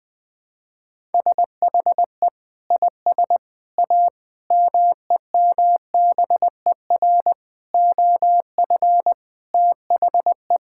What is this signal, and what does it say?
Morse code, heard as a keyed tone, about 20 wpm, 700 Hz. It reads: SHE IS A MEMBER OF THE
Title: Sentences with 5-Word Phrases - 20wpm